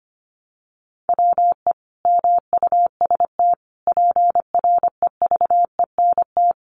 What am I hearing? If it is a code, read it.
Morse code, heard as a keyed tone, about 25 wpm, 700 Hz. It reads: WE MUST PREVENT
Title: Sentences from Top 1100 Words 25wpm